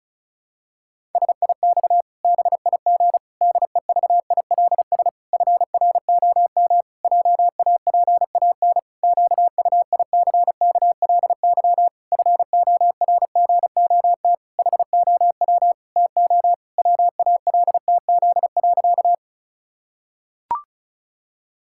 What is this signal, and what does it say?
Morse code, heard as a keyed tone, about 36 wpm, 700 Hz. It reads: SIX BIG DEVILS FROM JAPAN QUICKLY FORGOT HOW TO WALTZ.
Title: Pangram Challenge v2 - 35wpm